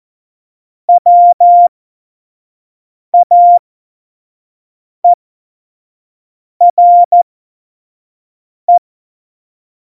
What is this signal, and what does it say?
Morse code, heard as a keyed tone, about 14 wpm, spaced as for 5 wpm, 700 Hz. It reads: WAERE
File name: Morsekurs / Witze / Witz 22. August 17, Tempo 25 F70